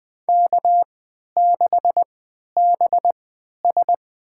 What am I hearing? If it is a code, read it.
Morse code, heard as a keyed tone, about 20 wpm, spaced as for 11 wpm, 700 Hz. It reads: K6BS
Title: USA Call Signs - 4 Characters - 3x Character Spacing - 20wpm